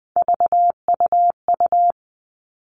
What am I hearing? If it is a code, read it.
Morse code, heard as a keyed tone, about 20 wpm, 700 Hz. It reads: VUU
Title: All Cut Numbers - 3-Digit 20wpm